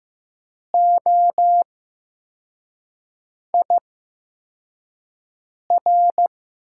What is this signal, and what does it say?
Morse code, heard as a keyed tone, about 15 wpm, spaced as for 4 wpm, 700 Hz. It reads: OIR
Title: USA Call Signs - 5 Characters - 8x Character Spacing - 15wpm